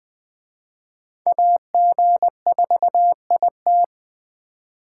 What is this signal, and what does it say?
Morse code, heard as a keyed tone, about 20 wpm, 700 Hz. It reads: AG4IT
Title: Speed Racing - USA 2x2 Call Signs 17wpm